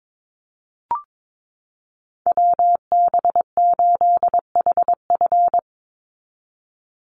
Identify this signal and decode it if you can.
Morse code, heard as a keyed tone, about 22 wpm, 700 Hz. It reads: WB8HF
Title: Call Signs Repeated 3x - 22wpm